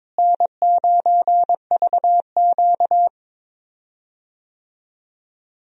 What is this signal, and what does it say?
Morse code, heard as a keyed tone, about 22 wpm, 700 Hz. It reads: N9VQ
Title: USA Call Signs - 4 Characters - Repeated 2x - 22wpm